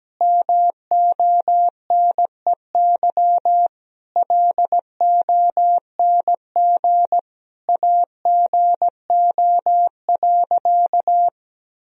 Morse code, heard as a keyed tone, about 17 wpm, 700 Hz. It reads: MONEY LONG AGO.
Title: Sentences from Top 1000 Words - Review Part 3 - 17wpm